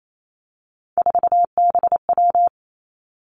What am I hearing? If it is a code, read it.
Morse code, heard as a keyed tone, about 28 wpm, 700 Hz. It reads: 4BW
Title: Three Challenging Letter-Number Combinations 28wpm